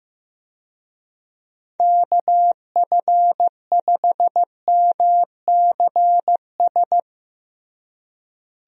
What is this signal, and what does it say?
Morse code, heard as a keyed tone, about 15 wpm, 700 Hz. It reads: KF5MCS
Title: USA Call Signs - 6 Characters - Repeated 2x - 15wpm